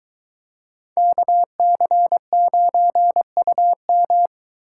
Morse code, heard as a keyed tone, about 22 wpm, 700 Hz. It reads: KC9UM